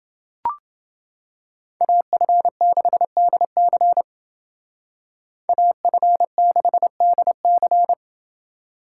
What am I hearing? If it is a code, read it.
Morse code, heard as a keyed tone, about 29 wpm, 700 Hz. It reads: AF6DC AF6DC